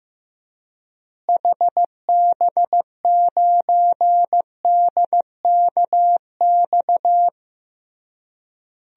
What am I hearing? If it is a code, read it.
Morse code, heard as a keyed tone, about 15 wpm, 700 Hz. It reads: HB9DKX
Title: DXCC-100 Call Signs - 6 Characters - Repeated 3x - 15wpm